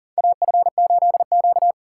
Morse code, heard as a keyed tone, about 40 wpm, 700 Hz. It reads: AF8Q
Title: USA Call Signs - 3x Character Spacing - 40wpm